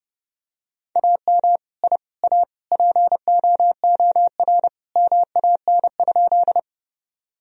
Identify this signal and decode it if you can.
Morse code, heard as a keyed tone, about 30 wpm, 700 Hz. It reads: AM I A POOR MAN?
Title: Sentences from Top 1000 Words - Review Part 2 - 30wpm